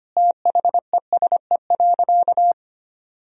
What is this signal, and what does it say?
Morse code, heard as a keyed tone, about 25 wpm, 700 Hz. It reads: THESE.
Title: Sentences from Top 1000 Words - Review Part 1 - 25wpm